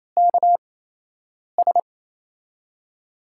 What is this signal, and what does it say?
Morse code, heard as a keyed tone, about 28 wpm, spaced as for 8 wpm, 700 Hz. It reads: KS
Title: USA Call Signs - 8x Character Spacing - 28wpm